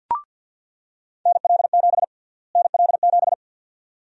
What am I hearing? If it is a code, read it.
Morse code, heard as a keyed tone, about 47 wpm, 700 Hz. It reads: NL7 NL7
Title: USA Call Sign Prefixes - Repeated 2x - 50wpm